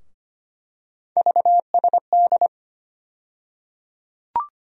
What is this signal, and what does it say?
Morse code, heard as a keyed tone, about 25 wpm, 700 Hz. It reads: VSD
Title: USA Call Sign Postfixes - 3 Letters 25wpm